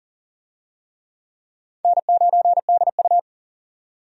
Morse code, heard as a keyed tone, about 40 wpm, 700 Hz. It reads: N9DU
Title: USA Call Signs - 4 Characters - 40wpm